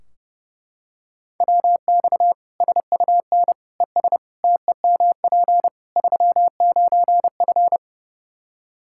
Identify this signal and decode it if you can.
Morse code, heard as a keyed tone, about 30 wpm, 700 Hz. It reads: WX SUN ES TEMP 39F
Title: Practice QSO - Exchange 2 - Info - 30wpm